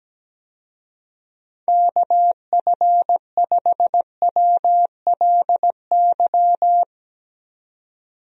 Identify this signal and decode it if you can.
Morse code, heard as a keyed tone, about 17 wpm, 700 Hz. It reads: KF5WLY
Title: USA Call Signs - 6 Characters - 17wpm